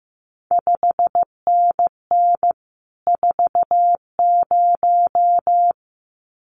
Morse code, heard as a keyed tone, about 15 wpm, 700 Hz. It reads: 5NN 40
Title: CQ WW DX Contest Exchange Encore - 15wpm